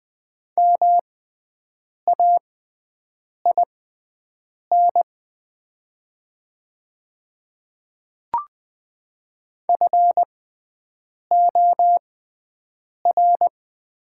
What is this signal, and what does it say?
Morse code, heard as a keyed tone, about 20 wpm, spaced as for 7 wpm, 700 Hz. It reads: MAIN FOR
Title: 6x Character Spacing - Top 300 Words - 20wpm